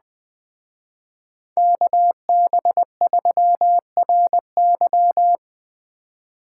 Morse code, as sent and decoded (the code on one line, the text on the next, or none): -.- -... ...-- .-. -.--
KB3RY